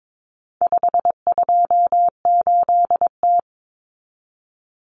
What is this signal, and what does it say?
Morse code, heard as a keyed tone, about 22 wpm, 700 Hz. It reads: 528T
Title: Common Cut Numbers - 4-Digit 22wpm